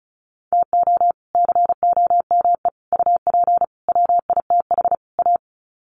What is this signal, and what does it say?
Morse code, heard as a keyed tone, about 34 wpm, 700 Hz. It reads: TO COME UP WITH A
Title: Sets of 5 Words 35wpm